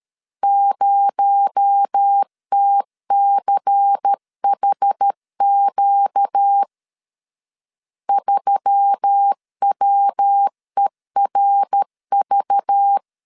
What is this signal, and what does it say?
Morse code, heard as a keyed tone, about 13 wpm, 800 Hz. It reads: 0TCHQ 3WERV
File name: LEZIONI / 6 PKX0 / 6 PKX0 12wpm 2231